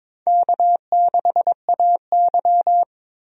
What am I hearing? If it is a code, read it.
Morse code, heard as a keyed tone, about 22 wpm, 700 Hz. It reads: K6AY
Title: USA Call Signs - 4 Characters - 22wpm